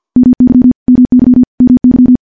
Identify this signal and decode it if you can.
Morse code, heard as a keyed tone, about 50 wpm, 250 Hz. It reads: MY MY MY
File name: morse / morse-training-50WPM